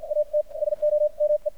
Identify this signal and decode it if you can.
Morse code, heard as a keyed tone, about 51 wpm, spaced as for 42 wpm, 600 Hz, with radio noise. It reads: AT SOME
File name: archived / CRSnet